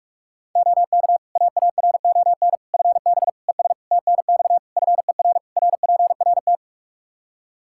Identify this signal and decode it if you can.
Morse code, heard as a keyed tone, about 46 wpm, 700 Hz. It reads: OK AARON FB ES TNX FER RPRT
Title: Practice QSO - Exchange 2 - Info - 45wpm